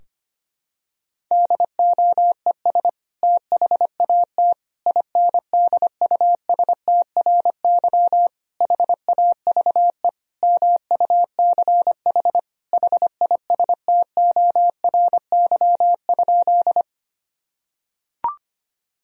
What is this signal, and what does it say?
Morse code, heard as a keyed tone, about 25 wpm, 700 Hz. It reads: DOES THAT INDUSTRY HAVE MUCH HISTORY?
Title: Sentences from Top 1000 Words - Review Part 4 - 25wpm